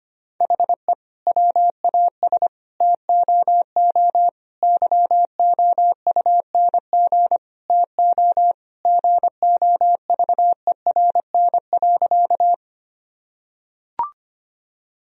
Morse code, heard as a keyed tone, about 25 wpm, 700 Hz. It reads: HE WAS TOO YOUNG TO GOVERN.
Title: Sentences from Top 1000 Words - Review Part 2 - 25wpm